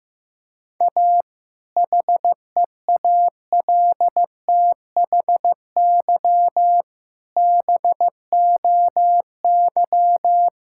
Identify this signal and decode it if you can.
Morse code, heard as a keyed tone, about 14 wpm, 700 Hz. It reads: A HEALTHY BOY
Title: Sentences from Top 1000 Words - Review Part 5 - 15wpm